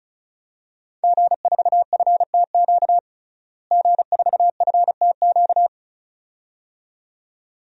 Morse code, heard as a keyed tone, about 36 wpm, 700 Hz. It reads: G4FTQ G4FTQ
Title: DXCC-100 Call Signs - 5 Characters - Repeated 2x - 35wpm